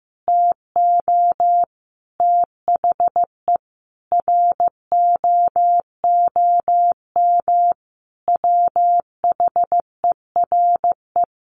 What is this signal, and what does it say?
Morse code, heard as a keyed tone, about 15 wpm, 700 Hz. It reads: TO THE ROOM WHERE
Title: Sentences from Top 1000 Words 15wpm